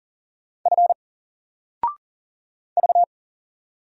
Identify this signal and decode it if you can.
Morse code, heard as a keyed tone, about 40 wpm, 700 Hz. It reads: F V
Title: Single Letter-Number 40wpm